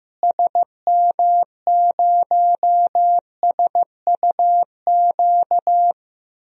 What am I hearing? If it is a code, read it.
Morse code, heard as a keyed tone, about 15 wpm, 700 Hz. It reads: SM0SUQ
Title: DXCC-100 Call Signs - 6 Characters - Repeated 3x - 15wpm